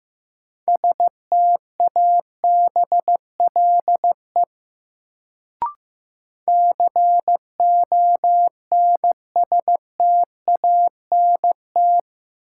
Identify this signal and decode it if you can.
Morse code, heard as a keyed tone, about 15 wpm, 700 Hz. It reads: STABLE CONSTANT
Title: Sentences from Top 1500 Words - New Word Drill - 15wpm